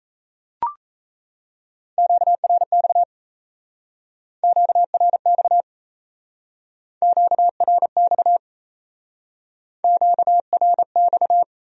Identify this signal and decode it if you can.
Morse code, heard as a keyed tone, about 37 wpm, 700 Hz. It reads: QRX QRX QRX QRX